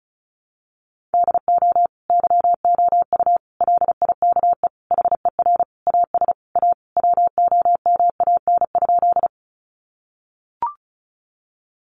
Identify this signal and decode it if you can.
Morse code, heard as a keyed tone, about 34 wpm, 700 Hz. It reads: DO YOU LIKE HER AS A WOMAN?
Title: Sentences from Top 1100 Words 35wpm